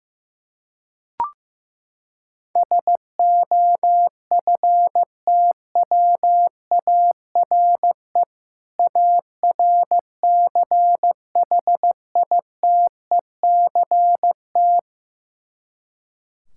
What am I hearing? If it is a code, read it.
Morse code, heard as a keyed tone, about 16 wpm, 700 Hz. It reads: SOFTWARE ARCHITECT
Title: Practice QSO - Exchange 3 - More Info - 15wpm